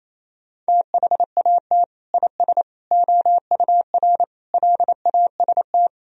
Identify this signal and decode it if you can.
Morse code, heard as a keyed tone, about 28 wpm, 700 Hz. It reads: THAT IS OUR LAST
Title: Sentences from Top 1000 Words - Review Part 4 - 28wpm